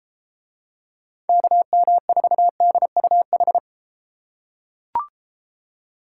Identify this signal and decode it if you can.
Morse code, heard as a keyed tone, about 31 wpm, 700 Hz. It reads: KM4DUH